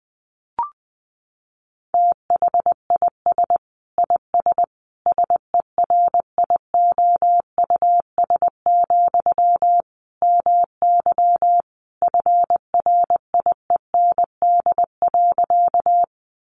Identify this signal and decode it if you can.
Morse code, heard as a keyed tone, about 20 wpm, 700 Hz. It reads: THIS IS SERIOUS, MY FRIEND.
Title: Sentences from Top 1100 Words 20wpm